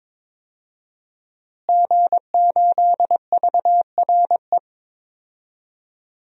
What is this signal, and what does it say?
Morse code, heard as a keyed tone, about 22 wpm, 700 Hz. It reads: G8VRE
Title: DXCC-100 Call Signs - 5 Characters - Repeated 3x - 22wpm